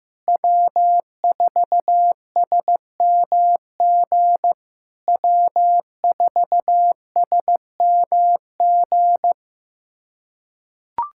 Morse code, heard as a keyed tone, about 15 wpm, 700 Hz. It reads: W4SMG W4SMG
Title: SOTA QSO Practice 15wpm